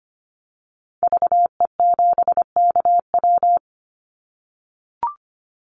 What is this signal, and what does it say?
Morse code, heard as a keyed tone, about 25 wpm, 700 Hz. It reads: VE7KW
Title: Call Signs - 25wpm